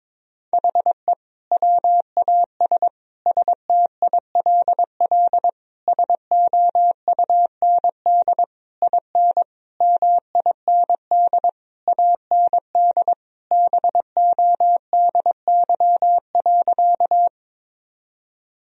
Morse code, heard as a keyed tone, about 22 wpm, 700 Hz. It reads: HE WAS STILL SOUND IN MIND AND BODY.
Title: Sentences from Top 1000 Words - Review Part 2 - 22wpm